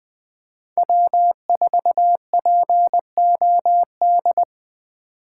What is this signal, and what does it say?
Morse code, heard as a keyed tone, about 20 wpm, 700 Hz. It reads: W4POD
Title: Speed Racing - USA 1x3 Call Signs 17wpm